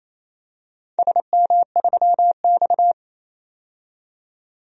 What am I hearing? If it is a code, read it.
Morse code, heard as a keyed tone, about 28 wpm, 700 Hz. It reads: SM3X
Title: DXCC-100 Call Signs - 4 Characters - 28wpm